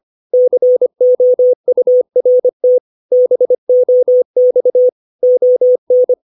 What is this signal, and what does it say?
Morse code, heard as a keyed tone, about 25 wpm, 500 Hz. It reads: COURT BOX ON
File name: 632 words / 632 Words 25WPM